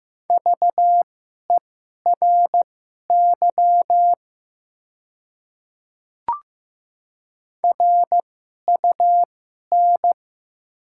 Morse code, heard as a keyed tone, about 15 wpm, spaced as for 11 wpm, 700 Hz. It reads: VERY RUN
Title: 2x Character Spacing - Top 300 Words - 15wpm